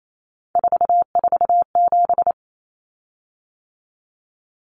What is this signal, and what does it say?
Morse code, heard as a keyed tone, about 28 wpm, 700 Hz. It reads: 447
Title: Common Cut Numbers - 3-Digit 28wpm